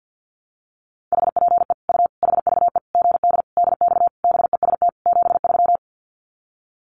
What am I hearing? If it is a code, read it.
Morse code, heard as a keyed tone, about 50 wpm, 700 Hz. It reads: HPE U HVE GD DX BEST 73
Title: Common QSO Phrases 50wpm